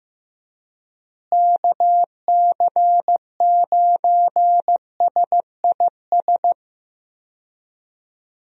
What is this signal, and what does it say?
Morse code, heard as a keyed tone, about 15 wpm, 700 Hz. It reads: KC9SIS
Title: USA Call Signs - 6 Characters - Repeated 2x - 15wpm